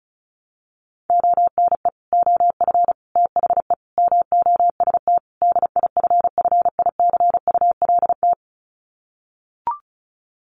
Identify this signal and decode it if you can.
Morse code, heard as a keyed tone, about 34 wpm, 700 Hz. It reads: ONE OF THE MOST DIFFICULT
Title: Sets of 5 Words Encore2 35wpm